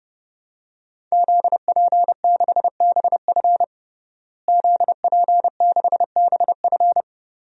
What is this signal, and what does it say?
Morse code, heard as a keyed tone, about 30 wpm, 700 Hz. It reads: ZP6BF ZP6BF
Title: DXCC-100 Call Signs - 5 Characters - Repeated 3x - 30wpm